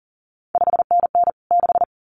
Morse code, heard as a keyed tone, about 41 wpm, 700 Hz. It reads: 5NN 6
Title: CQ WW DX Contest Exchange Encore - 40wpm